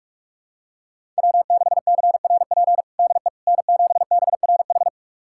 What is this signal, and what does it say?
Morse code, heard as a keyed tone, about 46 wpm, 700 Hz. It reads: W6CRP DE N7BRH
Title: USA Callsign DE Callsign - 5 Characters - 45wpm